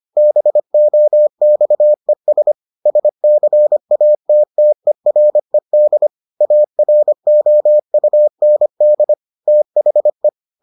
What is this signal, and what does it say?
Morse code, heard as a keyed tone, about 25 wpm, 600 Hz. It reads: BOXES SCATTERED AROUND THE